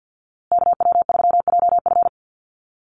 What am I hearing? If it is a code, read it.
Morse code, heard as a keyed tone, about 50 wpm, 700 Hz. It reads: KW3JF
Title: USA Call Signs Repeated 3x 50wpm